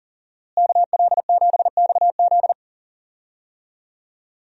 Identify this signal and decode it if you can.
Morse code, heard as a keyed tone, about 40 wpm, 700 Hz. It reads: KL7XZ
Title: DXCC-100 Call Signs - 5 Characters - Repeated 3x - 40wpm